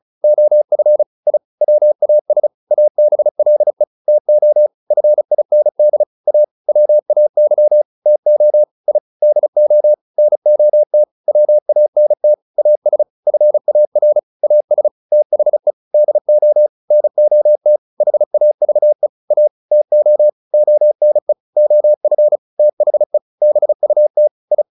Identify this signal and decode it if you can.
Morse code, heard as a keyed tone, about 34 wpm, 600 Hz. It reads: OF I WAS ABLE TO FIND A WAY TO I DO NOT WANT AS FAR AS THE DO NOT HAVE A TO ONE OF THE BUT I
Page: advanced-practice-files